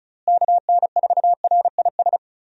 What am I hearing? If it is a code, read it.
Morse code, heard as a keyed tone, about 36 wpm, 700 Hz. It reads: KN4RIS
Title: USA Call Signs - 6 Characters - Repeated 3x - 35wpm